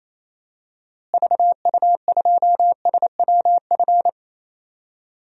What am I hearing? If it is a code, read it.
Morse code, heard as a keyed tone, about 28 wpm, 700 Hz. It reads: VU2SWF
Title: DXCC-100 Call Signs - 6 Characters - Repeated 2x - 28wpm